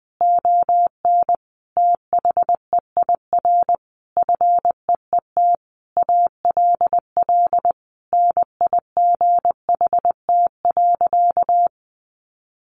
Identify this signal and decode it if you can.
Morse code, heard as a keyed tone, about 20 wpm, 700 Hz. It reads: ON THEIR FEET ALL NIGHT.
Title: Sentences from Top 1500 Words 20wpm